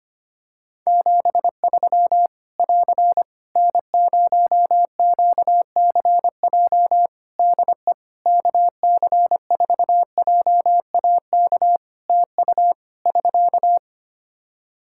Morse code, heard as a keyed tone, about 25 wpm, 700 Hz. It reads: 73 <AR> N0QCJ DE KC4JAK TU <SK>
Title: K1USN SST QSO Practice 25wpm